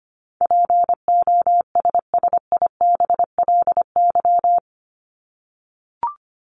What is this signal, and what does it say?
Morse code, heard as a keyed tone, about 25 wpm, 700 Hz. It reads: POSSIBLY
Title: LY Suffix - Top 100 Words with Base Words 25wpm